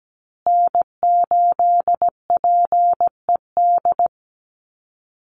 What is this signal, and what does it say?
Morse code, heard as a keyed tone, about 17 wpm, 700 Hz. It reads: N8PED
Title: USA Call Signs Repeated 2x 17wpm